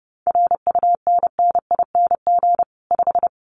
Morse code, heard as a keyed tone, about 30 wpm, 700 Hz. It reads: RUNNING 5
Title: Common QSO Phrases 30wpm